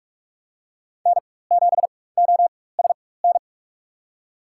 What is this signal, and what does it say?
Morse code, heard as a keyed tone, about 46 wpm, spaced as for 21 wpm, 700 Hz. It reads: N7OSN
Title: USA Call Signs - 5 Characters - 4x Character Spacing - 45wpm